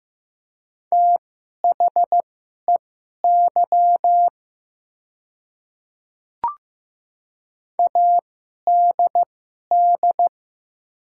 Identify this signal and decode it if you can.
Morse code, heard as a keyed tone, about 15 wpm, spaced as for 11 wpm, 700 Hz. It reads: THEY ADD
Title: 2x Character Spacing - Top 300 Words - 15wpm